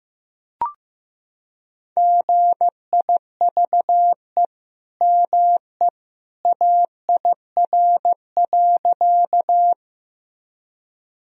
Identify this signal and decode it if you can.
Morse code, heard as a keyed tone, about 15 wpm, 700 Hz. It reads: GIVE ME AIR.
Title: Sentences from Top 1000 Words - Review Part 1 - 15wpm